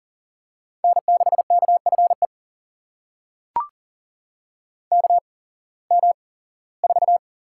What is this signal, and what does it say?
Morse code, heard as a keyed tone, about 39 wpm, spaced as for 21 wpm, 700 Hz. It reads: N6KFE KM4
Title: USA Call Signs - 8x Character Spacing - 40wpm